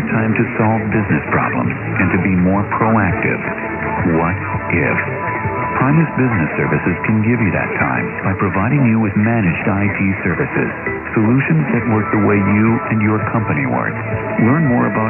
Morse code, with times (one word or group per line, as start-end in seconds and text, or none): none